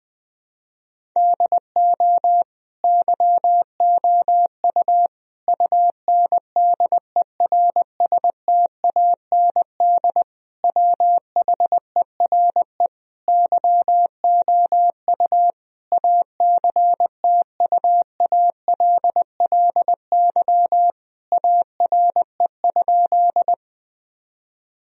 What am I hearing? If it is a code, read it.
Morse code, heard as a keyed tone, about 21 wpm, 700 Hz. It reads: DO YOU UNDERSTAND WHERE YOU ACTUALLY ARE?
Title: Sentences from Top 1000 Words - Review Part 4 - 20wpm